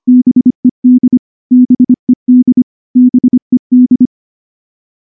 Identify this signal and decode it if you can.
Morse code, heard as a keyed tone, about 24 wpm, 250 Hz. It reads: BED BED BED
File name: morse / morse-training-25WPM